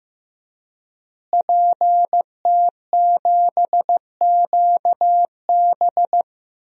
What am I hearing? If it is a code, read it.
Morse code, heard as a keyed tone, about 15 wpm, 700 Hz. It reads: PT7QB
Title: DXCC-100 Call Signs - 5 Characters - Repeated 3x - 15wpm